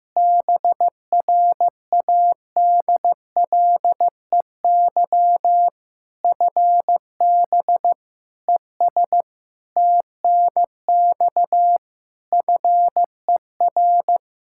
Morse code, heard as a keyed tone, about 15 wpm, 700 Hz. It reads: BRADLEY FB ES TNX FER
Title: Practice QSO - Exchange 2 - Info - 15wpm